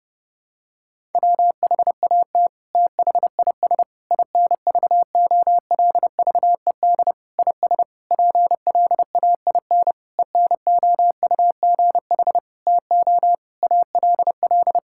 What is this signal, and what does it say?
Morse code, heard as a keyed tone, about 30 wpm, 700 Hz. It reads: WHAT THIS INVOLVED IS PLAIN ENOUGH TO ALL
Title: Sentences from Top 1000 Words - Review Part 5 - 30wpm